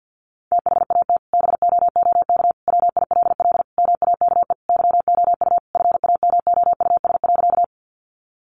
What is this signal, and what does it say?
Morse code, heard as a keyed tone, about 50 wpm, 700 Hz. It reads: THAT BOOK WILL MAKE YOU FAMOUS.
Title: Sentences from Top 1500 Words 50wpm